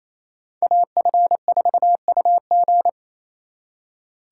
Morse code, heard as a keyed tone, about 32 wpm, 700 Hz. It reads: AF4UG
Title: Speed Racing - USA 2x2 Call Signs 28wpm